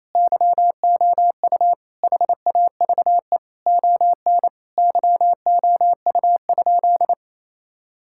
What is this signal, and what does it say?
Morse code, heard as a keyed tone, about 28 wpm, 700 Hz. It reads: YOU HAVE ON YOU?
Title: Sentences from Top 1000 Words - Review Part 5 - 28wpm